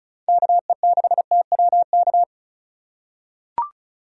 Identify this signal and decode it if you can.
Morse code, heard as a keyed tone, about 34 wpm, 700 Hz. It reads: KE6TWK
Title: USA Call Signs - 6 Characters - 2x Character Spacing - 35wpm